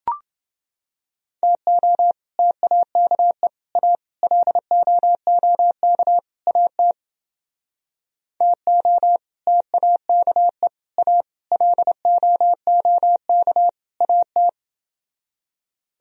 Morse code, heard as a keyed tone, about 29 wpm, 700 Hz. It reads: TO TAKE A LOOK AT TO TAKE A LOOK AT